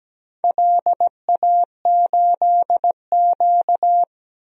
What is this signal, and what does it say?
Morse code, heard as a keyed tone, about 17 wpm, 700 Hz. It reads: LA8Q
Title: DXCC-100 Call Signs - 4 Characters - 17wpm